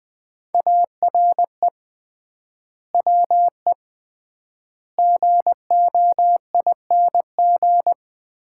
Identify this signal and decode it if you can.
Morse code, heard as a keyed tone, about 21 wpm, 700 Hz. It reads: ARE WE GOING
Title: 3x Word-Spacing - Sets of 4 Words - 20wpm